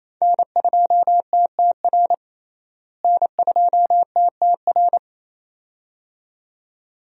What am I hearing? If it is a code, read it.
Morse code, heard as a keyed tone, about 27 wpm, 700 Hz. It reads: N2TTR N2TTR
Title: USA Call Signs - 5 Characters - Repeated 2x - 28wpm